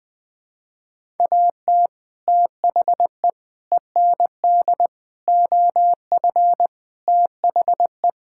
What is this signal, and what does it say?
Morse code, heard as a keyed tone, about 20 wpm, 700 Hz. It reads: AT THE END OF THE
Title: Sentences with 3-Word Phrases - 20wpm